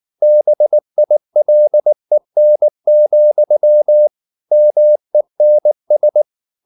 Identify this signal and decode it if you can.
Morse code, heard as a keyed tone, about 19 wpm, 600 Hz. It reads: BILEN, MENS
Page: Ferie 19 wpm